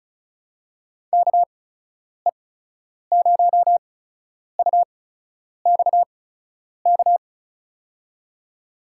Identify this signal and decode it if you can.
Morse code, heard as a keyed tone, about 36 wpm, spaced as for 10 wpm, 700 Hz. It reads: KE0UXK
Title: USA Call Signs - 6 Characters - 8x Character Spacing - 35wpm